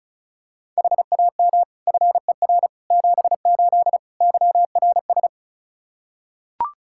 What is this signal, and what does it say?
Morse code, heard as a keyed tone, about 36 wpm, 700 Hz. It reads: HAM FER 78 YRS
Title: Practice QSO - Exchange 3 - More Info - 35wpm